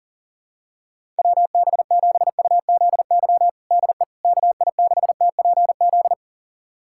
Morse code, heard as a keyed tone, about 37 wpm, 690 Hz. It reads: WB7UZY DE KI6TPZ
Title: USA Callsign DE Callsign - 6 Characters - 40wpm